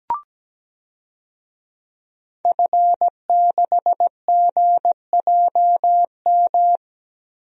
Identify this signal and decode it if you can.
Morse code, heard as a keyed tone, about 17 wpm, 700 Hz. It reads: F6GJM
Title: DXCC-100 Call Signs - 5 Characters - Repeated 3x - 17wpm